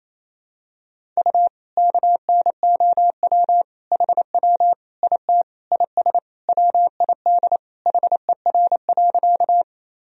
Morse code, heard as a keyed tone, about 28 wpm, 700 Hz. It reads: U KNOW HW IT IS WID HER.
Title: Sentences from Top 100 Words and Abbreviations 28wpm